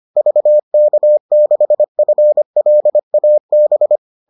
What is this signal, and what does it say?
Morse code, heard as a keyed tone, about 25 wpm, 600 Hz. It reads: VK6FLAB